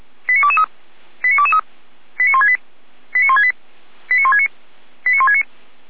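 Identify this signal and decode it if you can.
Five-tone selective call (ZVEI), heard as quick runs of five tones, tones of about 70 ms. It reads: 89223 89223 89168 89168 89169 89169
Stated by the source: ZVEI five-tone call 89167 received by SDR, fire brigade Feuerwehr Orlamünde (f85360).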